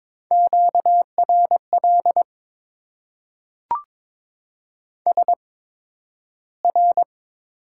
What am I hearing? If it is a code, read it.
Morse code, heard as a keyed tone, about 22 wpm, spaced as for 12 wpm, 700 Hz. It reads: QRL SR
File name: Top 90 QSO Elements - 8x Character Spacing - 22wpm